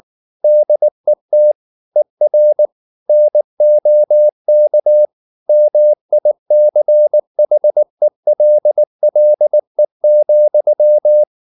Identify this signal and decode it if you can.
Morse code, heard as a keyed tone, about 19 wpm, 600 Hz. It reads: DET ER NOK MICHELLE,
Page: Ferie 19 wpm